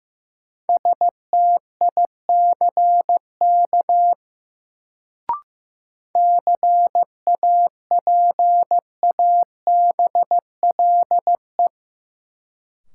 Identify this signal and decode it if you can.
Morse code, heard as a keyed tone, about 15 wpm, 700 Hz. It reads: STICK CAPABLE
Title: Sentences from Top 1200 Words - New Word Drill - 15wpm